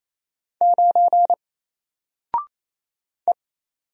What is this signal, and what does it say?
Morse code, heard as a keyed tone, about 27 wpm, 700 Hz. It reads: 9 E
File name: Single-Letter-Number-28wpm